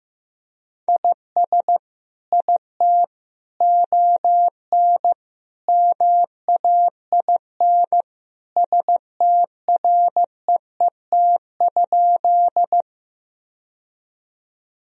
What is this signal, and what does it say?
Morse code, heard as a keyed tone, about 15 wpm, 700 Hz. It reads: IS IT ON MAIN STREET?
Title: Sentences from Top 1000 Words - Review Part 2 - 15wpm